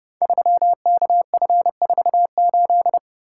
Morse code, heard as a keyed tone, about 30 wpm, 700 Hz. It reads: 3KF48
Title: Code Groups - 5 Letter:Numbers - 30wpm